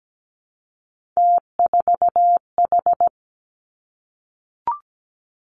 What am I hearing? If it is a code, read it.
Morse code, heard as a keyed tone, about 17 wpm, 700 Hz. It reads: T4H
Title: Three Challenging Letter-Number Combinations 17wpm